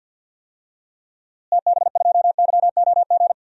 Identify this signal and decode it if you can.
Morse code, heard as a keyed tone, about 53 wpm, 700 Hz. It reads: TB2YOG
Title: DXCC-100 Call Signs - 6 Characters - Repeated 2x - 50wpm